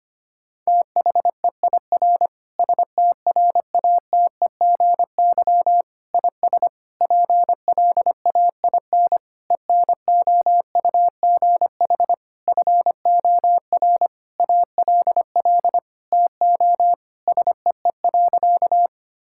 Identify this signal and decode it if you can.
Morse code, heard as a keyed tone, about 25 wpm, 700 Hz. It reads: THEIR STRATEGY IS PLAIN ENOUGH FOR ALL TO SEE.
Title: Sentences from Top 1000 Words - Review Part 5 - 25wpm